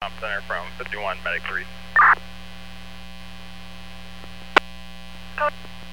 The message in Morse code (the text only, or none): none